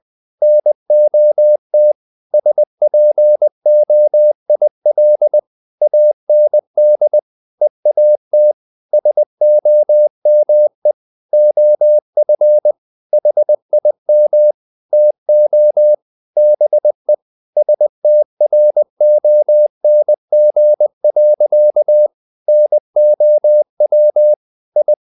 Morse code, heard as a keyed tone, about 20 wpm, 600 Hz. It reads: NOT SPOIL AND EAT SOME OF HIM TO BE STRONG. NOW I
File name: the old man and the sea part 2 20wpm